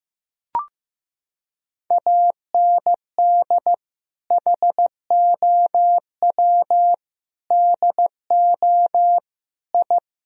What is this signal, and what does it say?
Morse code, heard as a keyed tone, about 15 wpm, 700 Hz. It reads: AND HOW DO I
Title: Sentences from Top 1000 Words - Review Part 3 - 15wpm